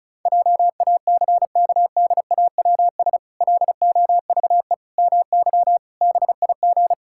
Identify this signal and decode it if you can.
Morse code, heard as a keyed tone, about 36 wpm, 700 Hz. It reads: JACKDAWS LOVE MY BIG
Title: Pangram Challenge v2 - 35wpm